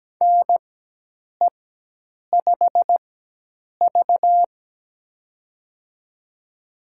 Morse code, heard as a keyed tone, about 17 wpm, spaced as for 8 wpm, 700 Hz. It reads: NE5V
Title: USA Call Signs - 4x Character Spacing - 17wpm